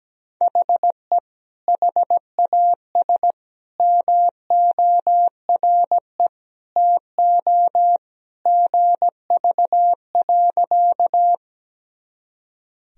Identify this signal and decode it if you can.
Morse code, heard as a keyed tone, about 17 wpm, 700 Hz. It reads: HE HAS MORE TO GV.
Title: Sentences from Top 200 Words and Abbreviations 17wpm